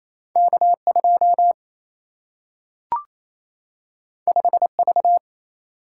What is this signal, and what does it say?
Morse code, heard as a keyed tone, about 28 wpm, 700 Hz. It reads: K2 5V
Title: Single-Letter Single-Number 28wpm